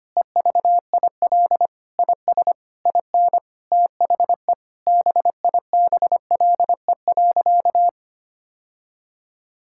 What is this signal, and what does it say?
Morse code, heard as a keyed tone, about 25 wpm, 700 Hz. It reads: EVIL IS IN THE BIBLE.
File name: practice / 25wpm / Sentences with 9-Word Phrases - 25wpm